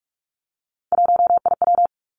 Morse code, heard as a keyed tone, about 46 wpm, 700 Hz. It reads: 1IW
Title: Three Challenging Letter-Number Combinations 45wpm